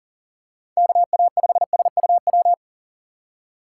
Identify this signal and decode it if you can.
Morse code, heard as a keyed tone, about 40 wpm, 700 Hz. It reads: KA5SUW
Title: USA Call Signs - 3x Character Spacing - 40wpm